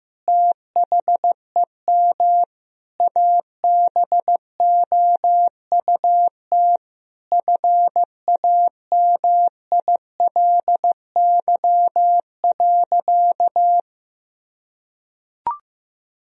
Morse code, heard as a keyed tone, about 15 wpm, 700 Hz. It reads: THEM ABOUT FAMILY.
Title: Sentences from Top 1000 Words - Review Part 3 - 15wpm